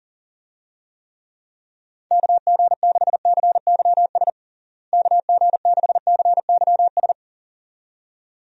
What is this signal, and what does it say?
Morse code, heard as a keyed tone, about 40 wpm, 700 Hz. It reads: KG6CYS KG6CYS
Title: USA Call Signs - 6 Characters - Repeated 2x - 40wpm